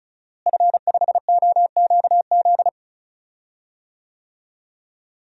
Morse code, heard as a keyed tone, about 34 wpm, 700 Hz. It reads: F5OQZ
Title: DXCC-100 Call Signs - 5 Characters - Repeated 2x - 35wpm